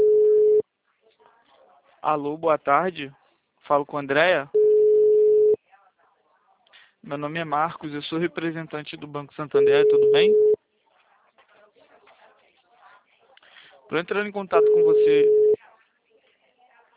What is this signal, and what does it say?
Ringback tone (Europe, 425 Hz), with speech.
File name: out- -2009- - - .54876